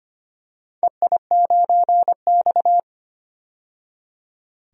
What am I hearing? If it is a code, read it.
Morse code, heard as a keyed tone, about 25 wpm, 700 Hz. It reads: EI9X
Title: DXCC-100 Call Signs - 4 Characters - 25wpm